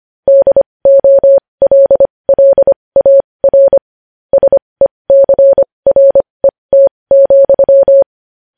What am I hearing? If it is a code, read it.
Morse code, heard as a keyed tone, about 25 wpm, spaced as for 20 wpm, 550 Hz. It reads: DOLLAR SECRET,
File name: podcast / News headlines-1x-25x20